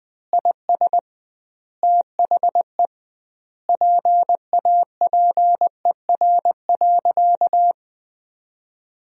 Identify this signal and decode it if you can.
Morse code, heard as a keyed tone, about 20 wpm, 700 Hz. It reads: IS THE PAPER.
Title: 2x Word-Spacing - Sentences from Top 300 Words - 20wpm